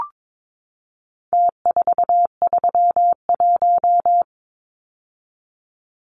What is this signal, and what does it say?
Morse code, heard as a keyed tone, about 22 wpm, 700 Hz. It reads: T431
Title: Numbers - Contest Style - 4-Digit 22wpm